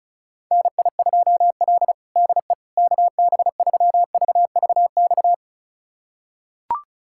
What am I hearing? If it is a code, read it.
Morse code, heard as a keyed tone, about 34 wpm, 700 Hz. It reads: NI2L DE KB3VVX
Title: USA Callsign DE Callsign 35wpm